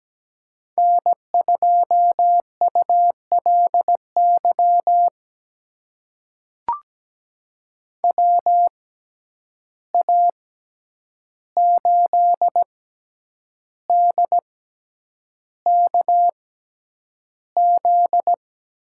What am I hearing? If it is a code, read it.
Morse code, heard as a keyed tone, about 17 wpm, spaced as for 8 wpm, 700 Hz. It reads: N2ULY WA8DKZ
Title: USA Call Signs - 6x Character Spacing - 17wpm